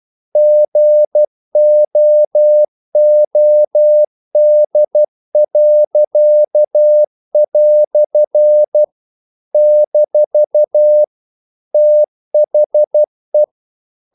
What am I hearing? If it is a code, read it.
Morse code, heard as a keyed tone, about 12 wpm, 600 Hz. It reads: GOOD." - THE
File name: CW / Practice / EnglishQuotations0312